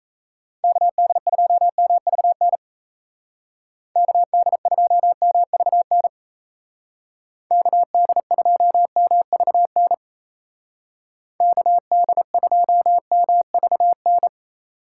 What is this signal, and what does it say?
Morse code, heard as a keyed tone, about 33 wpm, 700 Hz. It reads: KD2MVN KD2MVN KD2MVN KD2MVN
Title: Speed Racing - USA 2x3 Call Signs 28wpm